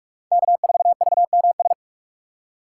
Morse code, heard as a keyed tone, about 41 wpm, 700 Hz. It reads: K4VMS
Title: USA Call Signs - 5 Characters - 6x Character Spacing - 45wpm